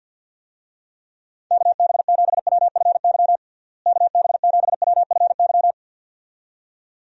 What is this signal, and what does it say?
Morse code, heard as a keyed tone, about 50 wpm, 700 Hz. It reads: KB7WFY KB7WFY
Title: USA Call Signs - 6 Characters - Repeated 2x - 50wpm